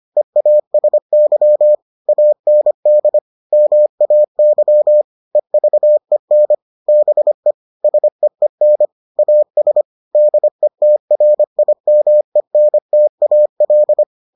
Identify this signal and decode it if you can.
Morse code, heard as a keyed tone, about 25 wpm, 600 Hz. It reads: EASY AND MAY EVEN BE SEEN AS DETRIMENTAL